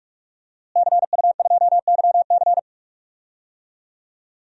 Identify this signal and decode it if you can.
Morse code, heard as a keyed tone, about 44 wpm, 700 Hz. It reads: CU2YC
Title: DXCC-100 Call Signs - 5 Characters - Repeated 2x - 45wpm